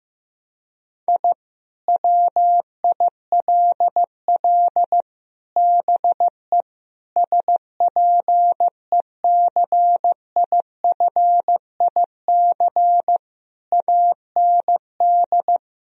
Morse code, heard as a keyed tone, about 15 wpm, 700 Hz. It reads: I WILL BE SPECIFIC AND
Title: Sentences from Top 1000 Words - Review Part 3 - 15wpm